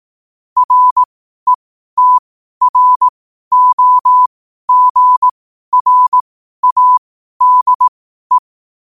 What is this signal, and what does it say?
Morse code, heard as a keyed tone, about 18 wpm, spaced as for 12 wpm, 1000 Hz. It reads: RETROGRADE
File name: qotdfiles / 12wpm